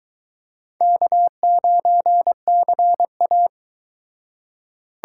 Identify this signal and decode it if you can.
Morse code, heard as a keyed tone, about 22 wpm, 700 Hz. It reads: K9CA